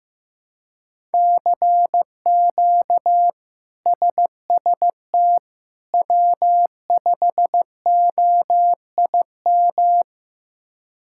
Morse code, heard as a keyed tone, about 15 wpm, 700 Hz. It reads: CQ SST W5OIM
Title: K1USN SST QSO Practice - 15wpm